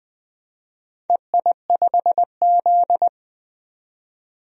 Morse code, heard as a keyed tone, about 20 wpm, 700 Hz. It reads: EI5Z
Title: DXCC-100 Call Signs - 4 Characters - 20wpm